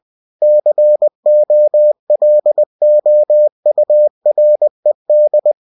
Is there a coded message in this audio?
Yes: Morse code from 0.4 to 5.5 s, heard as a keyed tone, about 20 wpm, 600 Hz. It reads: COLOURED